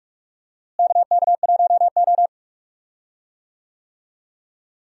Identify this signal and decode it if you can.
Morse code, heard as a keyed tone, about 42 wpm, 700 Hz. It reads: KK1O
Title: USA Call Signs - 4 Characters - Repeated 3x - 45wpm